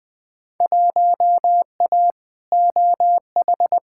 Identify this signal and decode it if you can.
Morse code, heard as a keyed tone, about 20 wpm, 700 Hz. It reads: 1A OH
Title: Field Day Exchanges - 20wpm